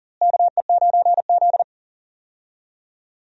Morse code, heard as a keyed tone, about 37 wpm, 700 Hz. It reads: KE9Z
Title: USA Call Signs - 4 Characters - Repeated 2x - 40wpm